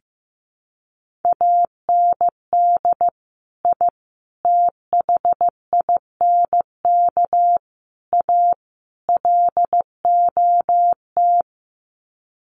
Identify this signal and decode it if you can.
Morse code, heard as a keyed tone, about 15 wpm, 700 Hz. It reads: AND I THINK A LOT
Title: Sets of 5 Words Encore2 15wpm